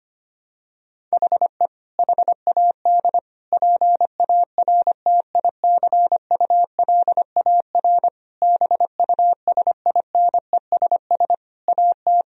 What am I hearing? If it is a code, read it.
Morse code, heard as a keyed tone, about 25 wpm, 700 Hz. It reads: HE HAD PARTICULAR BUSINESS AT
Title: Sentences from Top 1000 Words - Review Part 4 - 25wpm